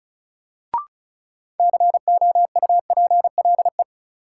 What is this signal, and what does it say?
Morse code, heard as a keyed tone, about 36 wpm, 700 Hz. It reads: COUPLE
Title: Sets of 2 Words - New Word Drill - 35wpm